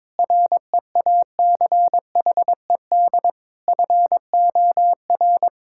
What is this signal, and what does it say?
Morse code, heard as a keyed tone, about 22 wpm, 700 Hz. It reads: REACHED FOR
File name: practice / 22wpm / Sets of 6 Words - 22wpm